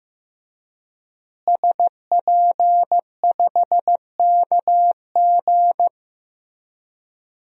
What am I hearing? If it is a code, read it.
Morse code, heard as a keyed tone, about 15 wpm, 700 Hz. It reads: SP5KG
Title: DXCC-100 Call Signs - 5 Characters - Repeated 2x - 15wpm